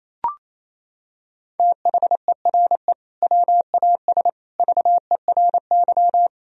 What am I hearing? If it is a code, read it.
Morse code, heard as a keyed tone, about 28 wpm, 700 Hz. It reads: THERE WAS VERY
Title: Sentences from Top 1000 Words - Review Part 1 - 28wpm